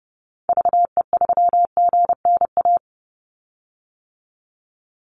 Morse code, heard as a keyed tone, about 30 wpm, 700 Hz. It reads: VE3GNA
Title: Call Signs - 30wpm